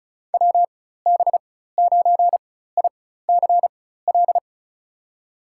Morse code, heard as a keyed tone, about 34 wpm, spaced as for 16 wpm, 700 Hz. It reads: WB9ICL
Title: USA Call Signs - 6 Characters - 4x Character Spacing - 35wpm